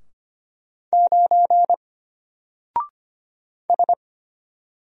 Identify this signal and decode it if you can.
Morse code, heard as a keyed tone, about 25 wpm, 700 Hz. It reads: 9 S
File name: Single-Letter-Number-25wpm